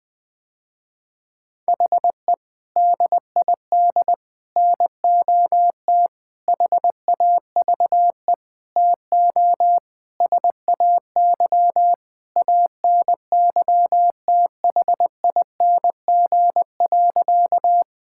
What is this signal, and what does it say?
Morse code, heard as a keyed tone, about 20 wpm, 700 Hz. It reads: HE DID NOT HAVE TO SAY ANYTHING.
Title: Sentences with 5-Word Phrases - Spoken Beginning - 20wpm